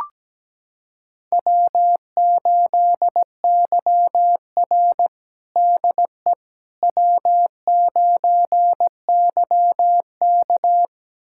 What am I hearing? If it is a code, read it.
Morse code, heard as a keyed tone, about 17 wpm, 700 Hz. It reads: W8YR DE W9YK
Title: USA Callsign DE Callsign - 4 Characters - 17wpm